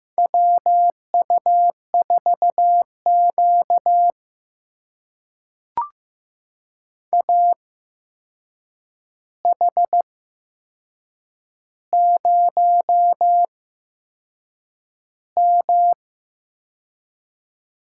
Morse code, heard as a keyed tone, about 15 wpm, spaced as for 6 wpm, 700 Hz. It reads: WU4Q AH0M
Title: USA Call Signs - 4 Characters - 8x Character Spacing - 15wpm